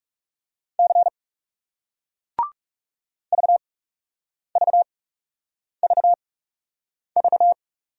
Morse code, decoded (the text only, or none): C VVVV